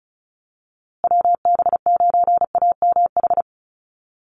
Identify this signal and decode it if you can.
Morse code, heard as a keyed tone, about 34 wpm, 700 Hz. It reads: WB9AMH
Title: USA Call Signs Repeated 3x 35wpm